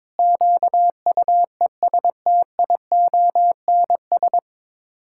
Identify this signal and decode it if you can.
Morse code, heard as a keyed tone, about 22 wpm, 700 Hz. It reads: QUESTIONS
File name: Sets of 9 Words - 22wpm